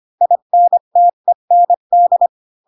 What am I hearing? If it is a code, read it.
Morse code, heard as a keyed tone, about 25 wpm, 700 Hz. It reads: INTEND